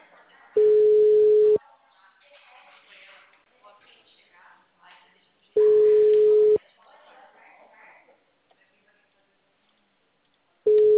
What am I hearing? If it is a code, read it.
Ringback tone (Europe, 425 Hz), with speech.
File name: out- -2004- - - .98730